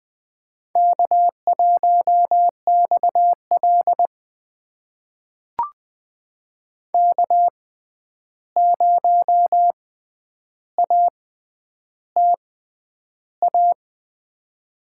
Morse code, heard as a keyed tone, about 21 wpm, spaced as for 10 wpm, 700 Hz. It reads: K1XL K0ATA
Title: USA Call Signs - 6x Character Spacing - 20wpm